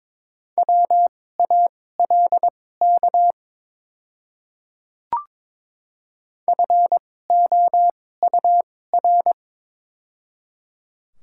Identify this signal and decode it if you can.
Morse code, heard as a keyed tone, about 22 wpm, spaced as for 16 wpm, 700 Hz. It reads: WALK FOUR
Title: 2x Character Spacing - Top 300 Words - 22wpm